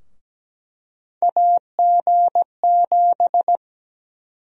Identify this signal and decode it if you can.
Morse code, heard as a keyed tone, about 17 wpm, 700 Hz. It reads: AG7
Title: USA Call Sign Prefixes - 17wpm